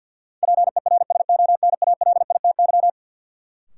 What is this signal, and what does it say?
Morse code, heard as a keyed tone, about 53 wpm, 700 Hz. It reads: PERSONALITY
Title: Sentences from Top 1400 Words - New Word Drill - 50wpm